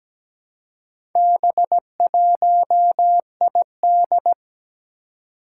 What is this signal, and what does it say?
Morse code, heard as a keyed tone, about 17 wpm, 700 Hz. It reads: B1ID
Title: Four Challenging Letter-Number Combinations 17wpm